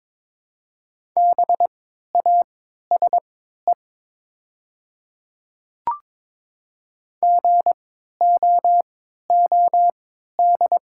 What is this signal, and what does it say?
Morse code, heard as a keyed tone, about 23 wpm, spaced as for 13 wpm, 700 Hz. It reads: BASE GOOD
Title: 3x Character Spacing - Top 300 Words - 22wpm